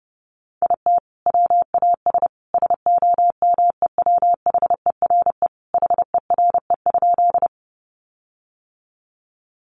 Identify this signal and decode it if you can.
Morse code, heard as a keyed tone, about 30 wpm, 700 Hz. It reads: IT WAS SOMEWHERE HERE?
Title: Sentences from Top 1400 Words 30wpm